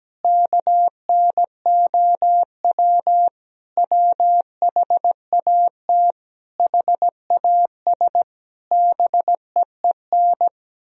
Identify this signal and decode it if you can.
Morse code, heard as a keyed tone, about 17 wpm, 700 Hz. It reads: KNOW WHAT HAS BEEN
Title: Sentences from Top 1000 Words - Review Part 1 - 17wpm